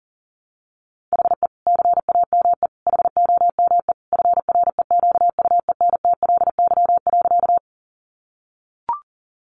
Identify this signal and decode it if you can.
Morse code, heard as a keyed tone, about 40 wpm, 700 Hz. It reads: HE CAME HOME FREQUENTLY.
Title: Sentences from Top 1300 Words 40wpm